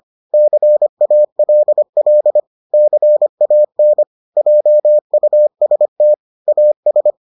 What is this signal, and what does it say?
Morse code, heard as a keyed tone, about 25 wpm, 600 Hz. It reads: CALL CAN JUST AS